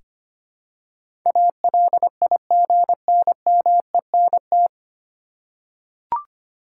Morse code, heard as a keyed tone, about 25 wpm, 700 Hz. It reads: ALIGNMENT
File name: practice / 25wpm / 9 Letter Words - First 3 Letters Given - 25wpm